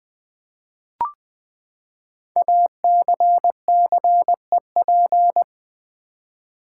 Morse code, heard as a keyed tone, about 20 wpm, 700 Hz. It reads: ACCEP
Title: Build A Word - 5 Letters At A Time - 10-Letter Words - 20wpm